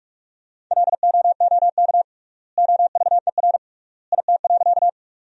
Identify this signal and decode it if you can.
Morse code, heard as a keyed tone, about 46 wpm, 690 Hz. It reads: LOOK OVER IT.
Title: 3x Word-Spacing - Sentences from Top 100 Words - 45wpm